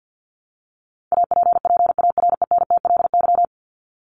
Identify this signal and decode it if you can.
Morse code, heard as a keyed tone, about 50 wpm, 700 Hz. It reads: APPARENTLY
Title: LY Suffix - Top 100 Words with Base Words 50wpm